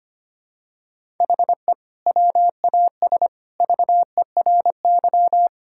Morse code, heard as a keyed tone, about 25 wpm, 700 Hz. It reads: HE WAS VERY
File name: practice / 25wpm / Sentences with 2-Word Phrases - 25wpm